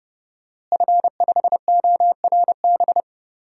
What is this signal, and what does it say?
Morse code, heard as a keyed tone, about 32 wpm, 700 Hz. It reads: F5ORB
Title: DXCC-100 Call Signs - 5 Characters - Repeated 2x - 30wpm